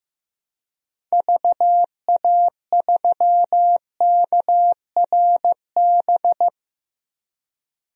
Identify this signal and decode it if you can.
Morse code, heard as a keyed tone, about 15 wpm, 700 Hz. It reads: VA3KRB
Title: DXCC-100 Call Signs - 6 Characters - 15wpm